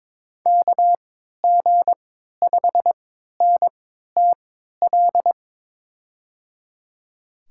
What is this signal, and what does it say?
Morse code, heard as a keyed tone, about 21 wpm, spaced as for 12 wpm, 700 Hz. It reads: KG5NTL
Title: USA Call Signs - 6 Characters - 3x Character Spacing - 22wpm